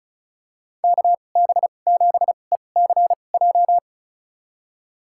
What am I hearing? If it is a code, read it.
Morse code, heard as a keyed tone, about 34 wpm, spaced as for 25 wpm, 700 Hz. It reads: KB7ECJ
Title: USA Call Signs - 2x Character Spacing - 35wpm